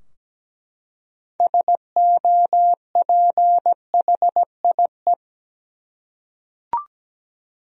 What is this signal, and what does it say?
Morse code, heard as a keyed tone, about 17 wpm, 700 Hz. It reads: SOPHIE
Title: USA Common Names - 17wpm